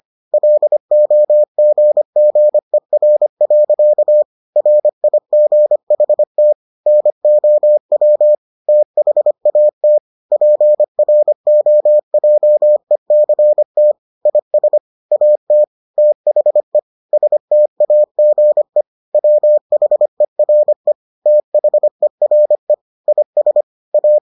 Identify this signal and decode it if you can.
Morse code, heard as a keyed tone, about 25 wpm, 600 Hz. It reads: LOGGER. RIGHT NOW THAT PROJECT IS AT THE STAGE WHERE THERE IS A